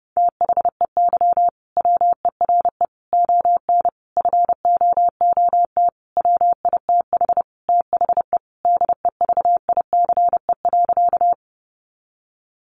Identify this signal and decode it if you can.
Morse code, heard as a keyed tone, about 30 wpm, 700 Hz. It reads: THEY WERE ON FOOT WITH THE DEVICE.
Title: Sentences from Top 1500 Words 30wpm